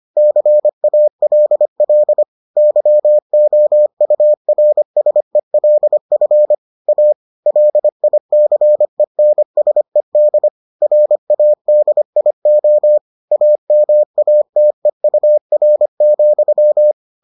Morse code, heard as a keyed tone, about 25 wpm, 600 Hz. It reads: CALL YOURSELF A LICENSED RADIO AMATEUR,